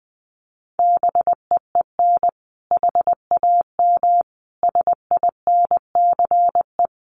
Morse code, heard as a keyed tone, about 19 wpm, 700 Hz. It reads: BEEN HAM SINCE
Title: Common QSO Phrases 20wpm